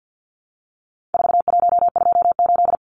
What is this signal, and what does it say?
Morse code, heard as a keyed tone, about 50 wpm, 700 Hz. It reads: 4128
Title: Numbers - 4-Digit 50wpm